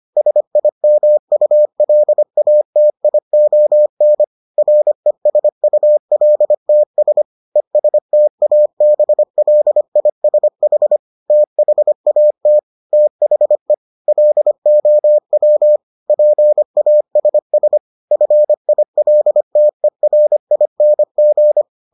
Morse code, heard as a keyed tone, about 25 wpm, 600 Hz. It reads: SIMULATION RESULTS ESTABLISH THAT THE LOW PASS FILTERING